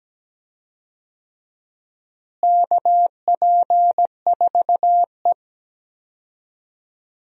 Morse code, heard as a keyed tone, about 17 wpm, 700 Hz. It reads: KP4E
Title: DXCC-100 Call Signs - 4 Characters - Repeated 3x - 17wpm